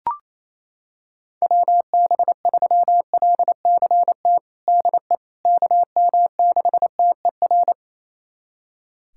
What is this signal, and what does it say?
Morse code, heard as a keyed tone, about 28 wpm, 700 Hz. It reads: WB3LCT DE KM6TER
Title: USA Callsign DE Callsign 28wpm